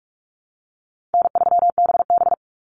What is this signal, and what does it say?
Morse code, heard as a keyed tone, about 46 wpm, 700 Hz. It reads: N3BB
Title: Easy Call Signs - 45wpm